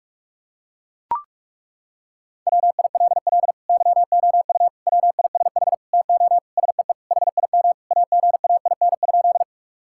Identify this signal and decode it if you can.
Morse code, heard as a keyed tone, about 41 wpm, 700 Hz. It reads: WILL YOU WISH TO SEE HIM AGAIN?
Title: Sentences from Top 1600 Words 45wpm